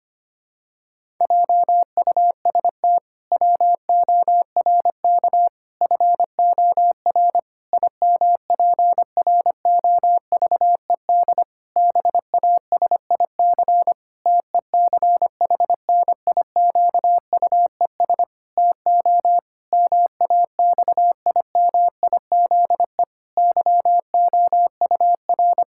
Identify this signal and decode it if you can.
Morse code, heard as a keyed tone, about 25 wpm, 700 Hz. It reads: JUST WORK FOR IMPROVED BASIC TECHNIQUES TO MAXIMIZE YOUR
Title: Pangram Challenge v2 - 25wpm